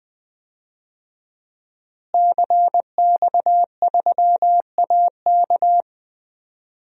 Morse code, heard as a keyed tone, about 20 wpm, 700 Hz. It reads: CX3AK
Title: DXCC-100 Call Signs - 5 Characters - Repeated 2x - 20wpm